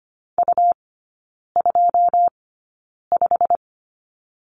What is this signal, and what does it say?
Morse code, heard as a keyed tone, about 25 wpm, spaced as for 9 wpm, 700 Hz. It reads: U25
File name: Lesson 17e - Letter u - Three Characters 25wpm